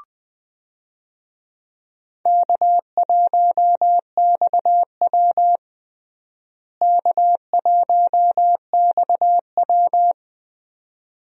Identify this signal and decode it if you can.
Morse code, heard as a keyed tone, about 21 wpm, 700 Hz. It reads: K1XW K1XW
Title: USA Call Signs - 4 Characters - Repeated 2x - 20wpm